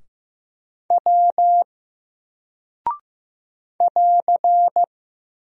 Morse code, heard as a keyed tone, about 15 wpm, 700 Hz. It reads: W <AR>
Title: Single Character (Letter:Number:Punctuation:Prosign) Alt 15wpm